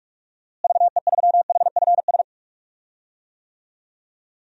Morse code, heard as a keyed tone, about 46 wpm, 700 Hz. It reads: VE3HFS
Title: DXCC-100 Call Signs - 6 Characters - Repeated 3x - 45wpm